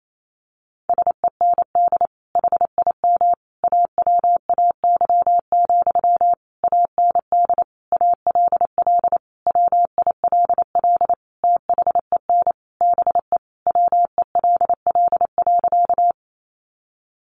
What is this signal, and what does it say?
Morse code, heard as a keyed tone, about 28 wpm, 700 Hz. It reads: SEND HIM AWAY, AND ALL WILL THEN BE WELL.
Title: Sentences from Top 1500 Words 28wpm